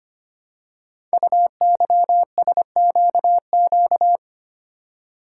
Morse code, heard as a keyed tone, about 25 wpm, 700 Hz. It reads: UYSQQ
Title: Code Groups - 5 Letters - 25wpm